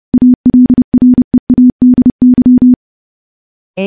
Morse code, heard as a keyed tone, about 30 wpm, 250 Hz. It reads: ALREADY
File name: morse / morse-training-30WPM